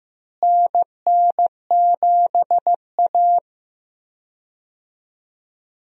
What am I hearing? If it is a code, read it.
Morse code, heard as a keyed tone, about 16 wpm, 700 Hz. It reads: NN7A
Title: USA Call Signs - 4 Characters - Repeated 2x - 15wpm